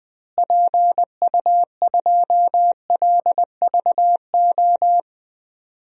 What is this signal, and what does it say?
Morse code, heard as a keyed tone, about 20 wpm, 700 Hz. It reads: PU2LVO
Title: DXCC-100 Call Signs - 6 Characters - Repeated 3x - 20wpm